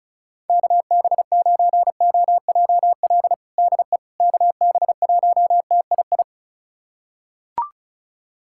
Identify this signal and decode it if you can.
Morse code, heard as a keyed tone, about 36 wpm, 700 Hz. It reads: KB9OJL DE KB1TII
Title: USA Callsign DE Callsign 35wpm